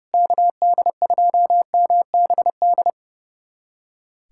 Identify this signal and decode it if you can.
Morse code, heard as a keyed tone, about 32 wpm, 700 Hz. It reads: KD2MBD
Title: USA Callsign DE Callsign - 6 Characters - 30wpm